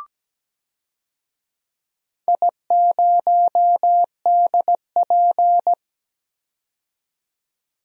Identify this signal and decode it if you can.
Morse code, heard as a keyed tone, about 17 wpm, 700 Hz. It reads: I0DP
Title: DXCC-100 Call Signs - 4 Characters - 17wpm